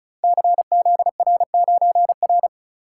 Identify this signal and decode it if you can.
Morse code, heard as a keyed tone, about 36 wpm, 700 Hz. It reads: CZR9R
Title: Code Groups - 5 Letter:Numbers - 35wpm